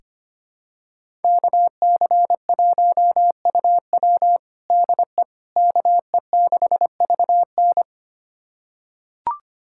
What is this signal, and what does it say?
Morse code, heard as a keyed tone, about 25 wpm, 700 Hz. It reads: KC1UW DE KE6VN
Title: USA Callsign DE Callsign - 5 Characters - 25wpm